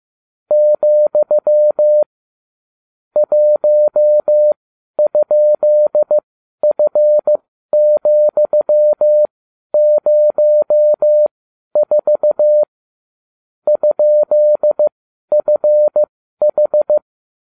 Morse code, heard as a keyed tone, about 15 wpm, spaced as for 11 wpm, 600 Hz. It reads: , 1?F,04 ?FH